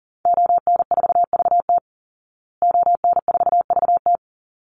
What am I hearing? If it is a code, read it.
Morse code, heard as a keyed tone, about 40 wpm, 700 Hz. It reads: ON4VT ON4VT
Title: Call Signs Repeated 3x - 40wpm